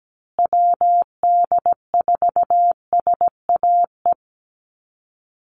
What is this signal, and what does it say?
Morse code, heard as a keyed tone, about 17 wpm, 700 Hz. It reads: WD4SAE
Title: USA Call Signs Repeated 3x 17wpm